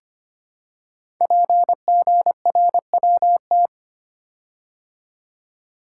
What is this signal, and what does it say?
Morse code, heard as a keyed tone, about 25 wpm, 700 Hz. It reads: PGRWT
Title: Code Groups - 5 Letters - 25wpm